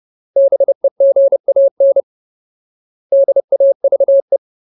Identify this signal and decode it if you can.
Morse code, heard as a keyed tone, about 30 wpm, 550 Hz. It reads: BEGAN DAVE
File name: Ultimate Word List / 30WPM Ultimate